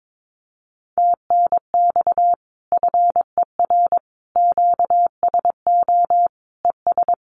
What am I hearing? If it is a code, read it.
Morse code, heard as a keyed tone, about 22 wpm, 700 Hz. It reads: TNX FER QSO ES
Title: Common QSO Phrases Encore 22wpm